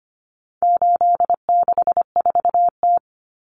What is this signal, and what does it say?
Morse code, heard as a keyed tone, about 25 wpm, 700 Hz. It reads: 864T
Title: Common Cut Numbers - 4-Digit 25wpm